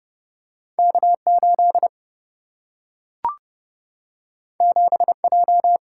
Morse code, heard as a keyed tone, about 30 wpm, 700 Hz. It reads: K8 7J
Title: Single-Letter Single-Number 30wpm